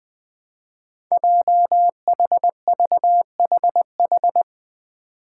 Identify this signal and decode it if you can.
Morse code, heard as a keyed tone, about 20 wpm, 700 Hz. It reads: JHVHH
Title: Five Challenging Letter-Number Combinations 20wpm